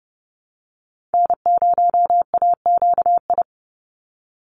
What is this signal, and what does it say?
Morse code, heard as a keyed tone, about 30 wpm, 700 Hz. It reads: N0AQI
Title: USA Call Signs 30wpm